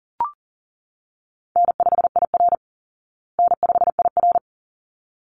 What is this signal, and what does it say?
Morse code, heard as a keyed tone, about 40 wpm, 700 Hz. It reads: N5IR N5IR
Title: Easy Call Signs Repeated 2x - 40wpm